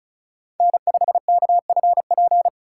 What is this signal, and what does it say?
Morse code, heard as a keyed tone, about 34 wpm, 700 Hz. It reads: N5KFP
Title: USA Call Signs - 5 Characters - 3x Character Spacing - 35wpm